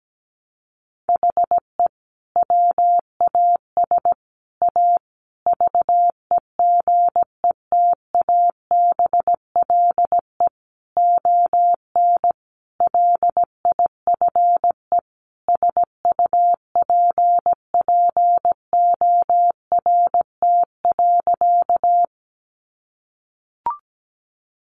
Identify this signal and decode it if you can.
Morse code, heard as a keyed tone, about 17 wpm, 700 Hz. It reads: HE WAS A VEGETABLE ON LIFE SUPPORT.
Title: Sentences from Top 1300 Words 17wpm